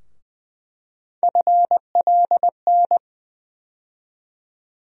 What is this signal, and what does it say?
Morse code, heard as a keyed tone, about 20 wpm, 700 Hz. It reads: FLN
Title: USA Call Sign Postfixes - 3 Letters 20wpm